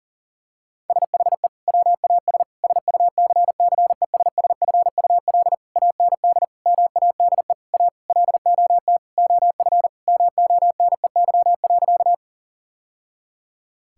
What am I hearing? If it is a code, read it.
Morse code, heard as a keyed tone, about 40 wpm, 700 Hz. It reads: SHE WAS SUCCESSFUL AND MADE A LOT OF MONEY.
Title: Sentences from Top 1000 Words - Review Part 6 - 40wpm